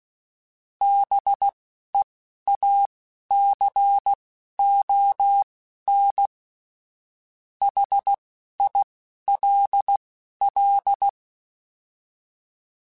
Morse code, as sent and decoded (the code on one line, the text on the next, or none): -... .- -.-. --- -. / .... .. .-.. .-..
BEACON HILL